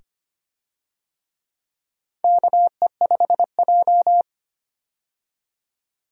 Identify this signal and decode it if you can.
Morse code, heard as a keyed tone, about 25 wpm, 700 Hz. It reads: KE5J
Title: USA Call Signs - 4 Characters - 25wpm